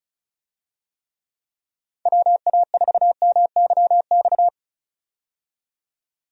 Morse code, heard as a keyed tone, about 36 wpm, 700 Hz. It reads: WA4MYX
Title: USA Call Signs - 6 Characters - 35wpm